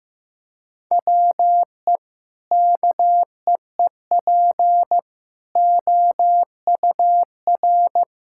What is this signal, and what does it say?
Morse code, heard as a keyed tone, about 15 wpm, 700 Hz. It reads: WE KEEP OUR
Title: Sentences from Top 1000 Words - Review Part 6 - 15wpm